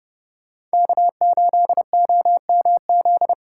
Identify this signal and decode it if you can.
Morse code, heard as a keyed tone, about 36 wpm, 700 Hz. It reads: K8OMZ